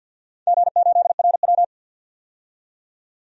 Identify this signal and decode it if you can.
Morse code, heard as a keyed tone, about 51 wpm, 700 Hz. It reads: G8RW
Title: DXCC-100 Call Signs - 4 Characters - 50wpm